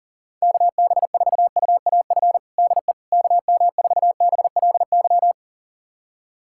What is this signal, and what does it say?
Morse code, heard as a keyed tone, about 40 wpm, 700 Hz. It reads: KB4UAF DE KM4BLY
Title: USA Callsign DE Callsign 40wpm